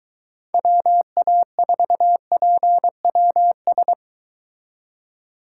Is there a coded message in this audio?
Yes: Morse code from 0.5 to 3.9 s, heard as a keyed tone, about 23 wpm, 700 Hz. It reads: WA4PWS